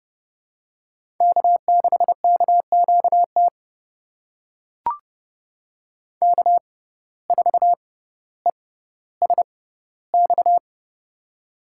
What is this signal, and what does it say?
Morse code, heard as a keyed tone, about 31 wpm, spaced as for 16 wpm, 700 Hz. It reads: K6KQT K4ESX
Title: USA Call Signs - 5 Characters - 6x Character Spacing - 30wpm